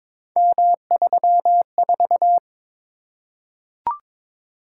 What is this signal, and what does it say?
Morse code, heard as a keyed tone, about 22 wpm, 700 Hz. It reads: M34
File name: practice / 22wpm / Three Letter-Number Combinations - 22wpm